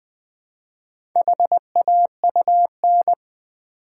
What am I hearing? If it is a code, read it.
Morse code, heard as a keyed tone, about 20 wpm, 700 Hz. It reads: HAUN
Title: Build A Word - 1 Letter At A Time - 5-Letter Words - 20wpm